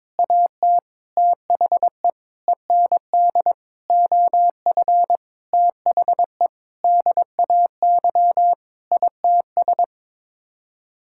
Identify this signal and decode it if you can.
Morse code, heard as a keyed tone, about 22 wpm, 700 Hz. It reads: AT THE END OF THE DAY ITS
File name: practice / 22wpm / Sets of 8 Words - 22wpm